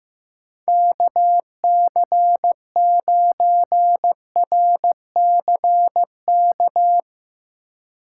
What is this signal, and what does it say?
Morse code, heard as a keyed tone, about 15 wpm, 700 Hz. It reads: KC9RCK
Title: USA Call Signs - 6 Characters - 2x Character Spacing - 15wpm